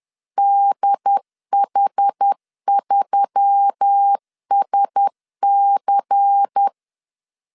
Morse code, heard as a keyed tone, about 10 wpm, 800 Hz. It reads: DH3SC